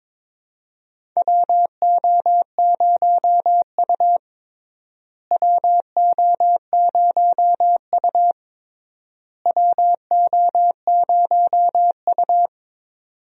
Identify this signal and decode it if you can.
Morse code, heard as a keyed tone, about 21 wpm, 700 Hz. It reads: WO0U WO0U WO0U
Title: USA Call Signs - 4 Characters - Repeated 3x - 22wpm